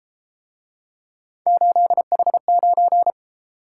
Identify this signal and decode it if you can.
Morse code, heard as a keyed tone, about 33 wpm, 700 Hz. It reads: 8H9